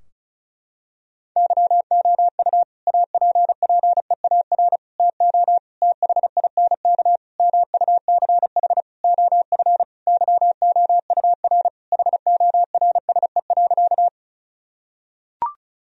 Morse code, heard as a keyed tone, about 36 wpm, 700 Hz. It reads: YOU APPEAR TO THINK MUCH OF YOUR HORSE.
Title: Sentences from Top 1000 Words - Review Part 2 - 35wpm